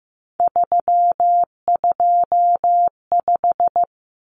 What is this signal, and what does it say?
Morse code, heard as a keyed tone, about 15 wpm, 700 Hz. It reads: 325
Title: Common Cut Numbers - 3-Digit 15wpm